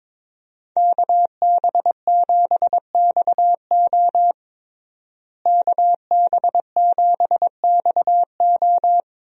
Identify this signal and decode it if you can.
Morse code, heard as a keyed tone, about 22 wpm, 700 Hz. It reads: KB7XO KB7XO
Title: USA Call Signs - 5 Characters - Repeated 3x - 22wpm